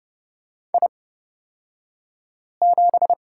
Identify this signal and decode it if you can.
Morse code, heard as a keyed tone, about 30 wpm, spaced as for 5 wpm, 700 Hz. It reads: I7
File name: Lesson 37d - Letter x - Two Characters 30wpm